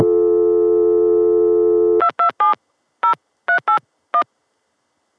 DTMF, heard as telephone touch tones, tones of about 100 ms. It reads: 227 0381